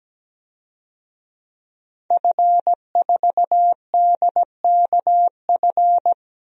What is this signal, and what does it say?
Morse code, heard as a keyed tone, about 17 wpm, 700 Hz. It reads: F4DKF
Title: DXCC-100 Call Signs - 5 Characters - Repeated 2x - 17wpm